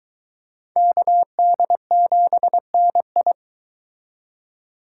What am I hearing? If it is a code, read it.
Morse code, heard as a keyed tone, about 23 wpm, 700 Hz. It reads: KD7NI